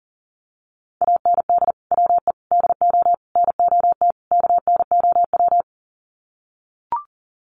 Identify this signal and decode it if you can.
Morse code, heard as a keyed tone, about 40 wpm, 700 Hz. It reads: AND WE DO NOT KNOW
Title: Sets of 5 Words Encore 40wpm